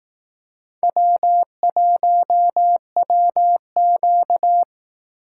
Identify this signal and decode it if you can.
Morse code, heard as a keyed tone, about 18 wpm, 700 Hz. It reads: W1WQ
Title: Speed Racing - USA 1x2 Call Signs 15wpm